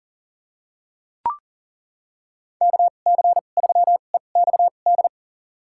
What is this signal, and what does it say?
Morse code, heard as a keyed tone, about 40 wpm, spaced as for 29 wpm, 700 Hz. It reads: KC3EXD
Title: USA Call Signs - 6 Characters - 2x Character Spacing - 40wpm